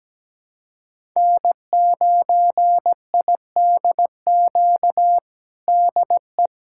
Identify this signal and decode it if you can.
Morse code, heard as a keyed tone, about 17 wpm, 700 Hz. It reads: N9IDQ DE
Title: USA Callsign DE Callsign 17wpm